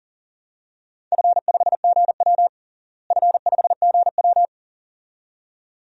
Morse code, heard as a keyed tone, about 40 wpm, 690 Hz. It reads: F5GW F5GW
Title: DXCC-100 Call Signs - 4 Characters - Repeated 2x - 40wpm